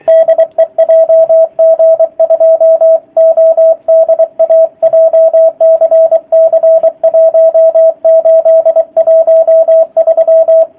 Morse code, heard as a keyed tone, about 24 wpm, 650 Hz, with radio noise. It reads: DE JG2ODA JCC 1813
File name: jh8cbh / 2019cw / sp27- w